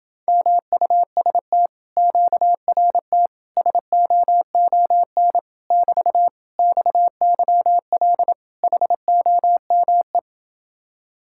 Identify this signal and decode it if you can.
Morse code, heard as a keyed tone, about 28 wpm, 700 Hz. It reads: MUST QRT SOON <BT> XYL HOME